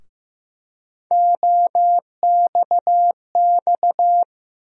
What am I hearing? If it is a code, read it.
Morse code, heard as a keyed tone, about 15 wpm, 700 Hz. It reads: OXX
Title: USA Call Sign Postfixes - 3 Letters 15wpm